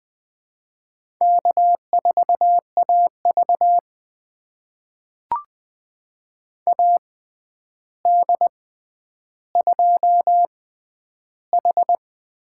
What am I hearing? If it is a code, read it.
Morse code, heard as a keyed tone, about 20 wpm, spaced as for 10 wpm, 700 Hz. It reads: K4AV AD2H
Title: USA Call Signs - 4 Characters - 6x Character Spacing - 20wpm